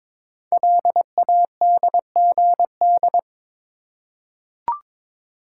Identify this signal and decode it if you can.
Morse code, heard as a keyed tone, about 22 wpm, 700 Hz. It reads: LADGD
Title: Code Groups - 5 Letters - 22wpm